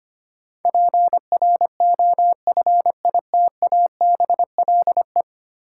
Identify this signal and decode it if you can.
Morse code, heard as a keyed tone, about 25 wpm, 700 Hz. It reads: PROFITABLE
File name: practice / 25wpm / Ten Letter Words - 25wpm